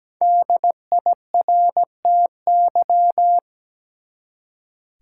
Morse code, heard as a keyed tone, about 17 wpm, 700 Hz. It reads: DIRTY
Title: Five Letter Words - 17wpm